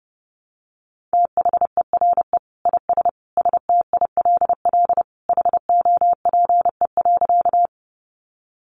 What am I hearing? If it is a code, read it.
Morse code, heard as a keyed tone, about 30 wpm, 700 Hz. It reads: THERE IS STILL HOPE.
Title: Sentences from Top 1000 Words 30wpm